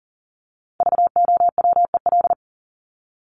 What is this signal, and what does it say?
Morse code, heard as a keyed tone, about 40 wpm, 700 Hz. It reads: VOWEL
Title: Top 500 Words 40wpm